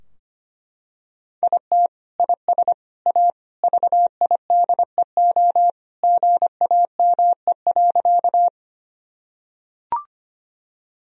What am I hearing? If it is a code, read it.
Morse code, heard as a keyed tone, about 25 wpm, 700 Hz. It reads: IT IS A VIDEO GAME.
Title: Sentences from Top 1000 Words - Review Part 4 - 25wpm